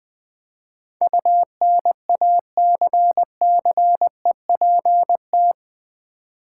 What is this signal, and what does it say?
Morse code, heard as a keyed tone, about 20 wpm, 700 Hz. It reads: UNACCEPT
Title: Build A Word - 4 Letters At A Time - 12-Letter Words - 20wpm